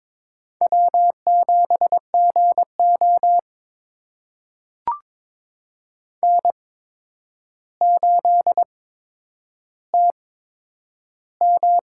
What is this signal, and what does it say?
Morse code, heard as a keyed tone, about 22 wpm, spaced as for 9 wpm, 700 Hz. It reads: W7GO N8TM
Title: USA Call Signs - 4 Characters - 8x Character Spacing - 22wpm